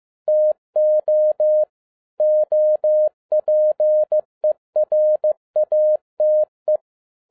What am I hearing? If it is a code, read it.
Morse code, heard as a keyed tone, about 15 wpm, 610 Hz. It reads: TO OPERATE
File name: 2011-07-mp3 / n12 T8XO 15wpm